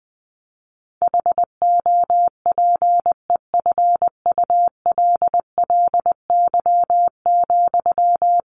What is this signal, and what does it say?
Morse code, heard as a keyed tone, about 19 wpm, 700 Hz. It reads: HOPEFULLY,
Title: Sentences from Top 1500 Words 20wpm